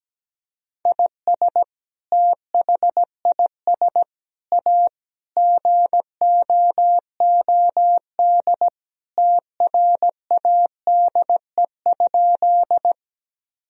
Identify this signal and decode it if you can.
Morse code, heard as a keyed tone, about 17 wpm, 700 Hz. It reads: IS THIS A GOOD TRADE?
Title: Sentences from Top 1000 Words - Review Part 4 - 17wpm